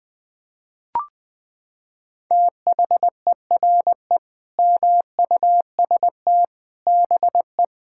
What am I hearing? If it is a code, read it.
Morse code, heard as a keyed tone, about 20 wpm, 700 Hz. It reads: THERE MUST BE
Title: Sentences from Top 1000 Words - Review Part 6 - 20wpm